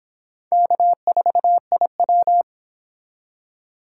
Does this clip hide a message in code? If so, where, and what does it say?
Yes: Morse code from 0.5 to 2.4 s, heard as a keyed tone, about 24 wpm, 700 Hz. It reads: K4IW